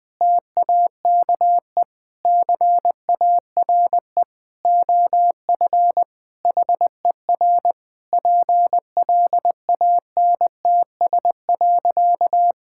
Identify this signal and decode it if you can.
Morse code, heard as a keyed tone, about 20 wpm, 700 Hz. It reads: TAKE CARE OF HER PLANTS.
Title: Sentences with 4-Word Phrases - 20wpm